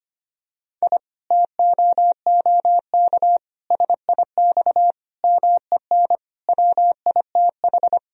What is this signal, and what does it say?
Morse code, heard as a keyed tone, about 25 wpm, 700 Hz. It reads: I TOOK SIX MEN WITH
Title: Sentences from Top 1000 Words - Review Part 2 - 25wpm